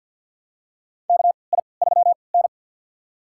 Morse code, heard as a keyed tone, about 47 wpm, spaced as for 28 wpm, 700 Hz. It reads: KI3N
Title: USA Call Signs - 3x Character Spacing - 50wpm